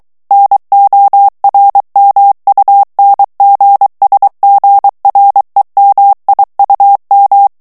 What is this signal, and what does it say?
Morse code, heard as a keyed tone, about 23 wpm, 800 Hz. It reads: NORMUNGSGREMIUM
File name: morse mp3 / ALLEMAND texte clair 1 vitesse 22 mots minute